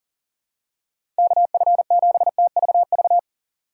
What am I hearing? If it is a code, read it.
Morse code, heard as a keyed tone, about 37 wpm, 700 Hz. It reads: KF7TVV
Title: USA Call Signs - 6 Characters - 4x Character Spacing - 40wpm